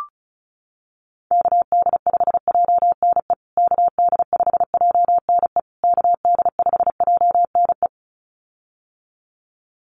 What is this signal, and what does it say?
Morse code, heard as a keyed tone, about 34 wpm, 700 Hz. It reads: KD5JNE KD5JNE KD5JNE
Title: USA Call Signs Repeated 3x 35wpm